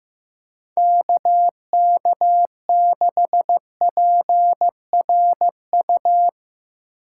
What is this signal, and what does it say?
Morse code, heard as a keyed tone, about 17 wpm, 700 Hz. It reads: KK6PRU
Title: Speed Racing - USA 2x3 Call Signs 15wpm